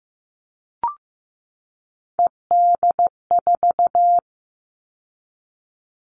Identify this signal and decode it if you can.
Morse code, heard as a keyed tone, about 15 wpm, 700 Hz. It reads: ED4
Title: All Cut Numbers - 3-Digit 15wpm